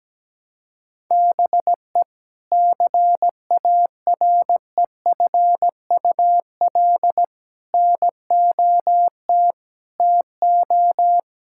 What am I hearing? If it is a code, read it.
Morse code, heard as a keyed tone, about 17 wpm, 700 Hz. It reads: BE CAREFUL NOT TO
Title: Sentences from Top 1600 Words 17wpm